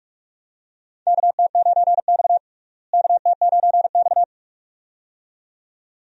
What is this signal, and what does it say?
Morse code, heard as a keyed tone, about 46 wpm, 700 Hz. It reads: KT9X KT9X
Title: USA Call Signs - 4 Characters - Repeated 2x - 45wpm